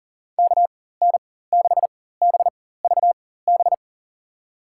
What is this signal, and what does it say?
Morse code, heard as a keyed tone, about 40 wpm, spaced as for 19 wpm, 700 Hz. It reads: KN6BVB
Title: USA Call Signs - 4x Character Spacing - 40wpm